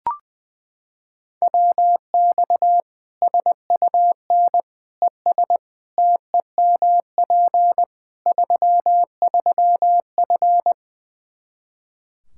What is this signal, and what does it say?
Morse code, heard as a keyed tone, about 21 wpm, 700 Hz. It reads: WX SUN ES TEMP 33F
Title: K1USN SST QSO Practice 20wpm